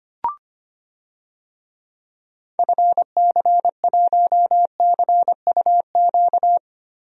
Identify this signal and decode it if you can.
Morse code, heard as a keyed tone, about 25 wpm, 700 Hz. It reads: FC1CUQ
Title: DXCC-100 Call Signs - 6 Characters - Repeated 3x - 25wpm